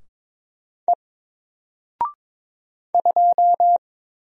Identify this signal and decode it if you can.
Morse code, heard as a keyed tone, about 22 wpm, 700 Hz. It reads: E 2
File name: Single-Letter-Number-22wpm